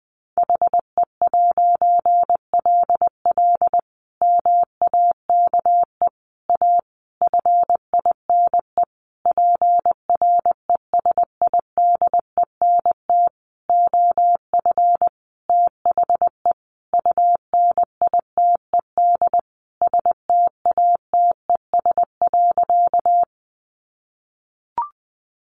Morse code, heard as a keyed tone, about 19 wpm, 700 Hz. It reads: HE'LL MAKE A FINE PRESIDENT OF THE UNITED STATES.
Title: A Sound Of Thunder 20wpm